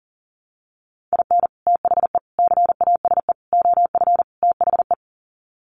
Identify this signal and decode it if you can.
Morse code, heard as a keyed tone, about 40 wpm, 700 Hz. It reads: IN THE CASE OF THE
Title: Sets of 5 Words 40wpm